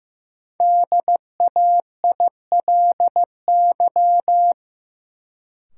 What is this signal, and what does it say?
Morse code, heard as a keyed tone, about 15 wpm, 700 Hz. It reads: DAILY
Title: Sentences from Top 1100 Words - New Word Drill - 15wpm